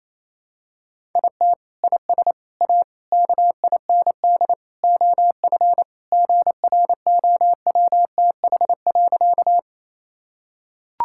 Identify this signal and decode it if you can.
Morse code, heard as a keyed tone, about 28 wpm, 700 Hz. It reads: IT IS A KIND OF GROWTH.
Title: Sentences from Top 1000 Words - Review Part 5 - 28wpm